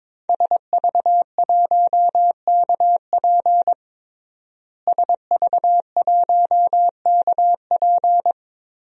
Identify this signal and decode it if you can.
Morse code, heard as a keyed tone, about 22 wpm, 700 Hz. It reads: SV1KP SV1KP
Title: DXCC-100 Call Signs - 5 Characters - Repeated 2x - 22wpm